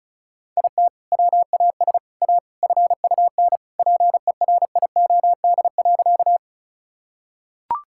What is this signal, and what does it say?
Morse code, heard as a keyed tone, about 34 wpm, 700 Hz. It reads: IT WAS A FUN PERIOD.
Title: Sentences from Top 1000 Words - Review Part 5 - 35wpm